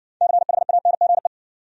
Morse code, heard as a keyed tone, about 58 wpm, 700 Hz. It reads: CHANGE